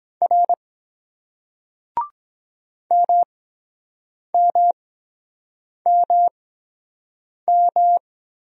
Morse code, heard as a keyed tone, about 23 wpm, spaced as for 7 wpm, 700 Hz. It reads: R MMMM